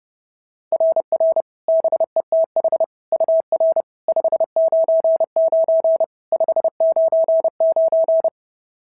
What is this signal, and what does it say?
Morse code, heard as a keyed tone, about 26 wpm, 650 Hz. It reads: RR BETH UR 599 599
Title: SOTA QSO Practice 30wpm